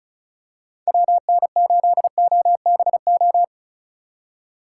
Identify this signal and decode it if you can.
Morse code, heard as a keyed tone, about 35 wpm, 700 Hz. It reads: WN8OBO
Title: USA Call Signs - 6 Characters - 6x Character Spacing - 35wpm